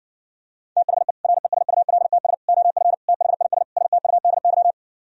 Morse code, heard as a keyed tone, about 59 wpm, 700 Hz. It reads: THE RESULTS OF THIS STUDY